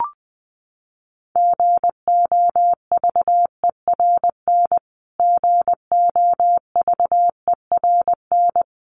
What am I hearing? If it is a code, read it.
Morse code, heard as a keyed tone, about 20 wpm, 700 Hz. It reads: GOVERN GOVERN
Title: Top 500 Words Repeated 20wpm